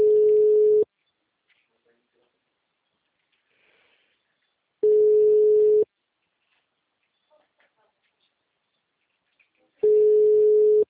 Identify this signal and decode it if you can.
Ringback tone (Europe, 425 Hz), with speech.
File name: out- -2011- - - .82821